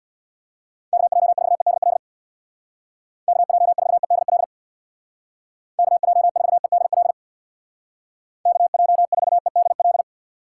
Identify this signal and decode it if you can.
Morse code, heard as a keyed tone, about 65 wpm, 700 Hz. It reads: KJ4EDL KJ4EDL KJ4EDL KJ4EDL